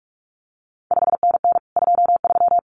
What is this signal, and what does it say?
Morse code, heard as a keyed tone, about 46 wpm, 700 Hz. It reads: 5NN 23
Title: CQ WW DX Contest Exchange Encore - 45wpm